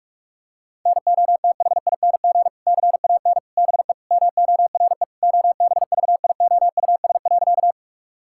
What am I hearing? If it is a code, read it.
Morse code, heard as a keyed tone, about 46 wpm, 700 Hz. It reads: NOTHING CAN BE MORE OBVIOUS.
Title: Sentences from Top 1600 Words 45wpm